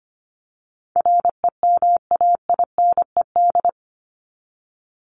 Morse code, heard as a keyed tone, about 25 wpm, 700 Hz. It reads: REMAINED
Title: ED Suffix - Top 100 Words with Base Words 25wpm